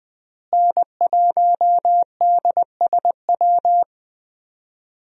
Morse code, heard as a keyed tone, about 20 wpm, 700 Hz. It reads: N1DSW
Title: USA Call Signs - 5 Characters - Repeated 3x - 20wpm